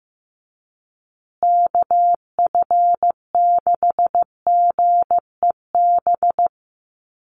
Morse code, heard as a keyed tone, about 15 wpm, 700 Hz. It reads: KF6GEB
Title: USA Call Signs Repeated 3x 15wpm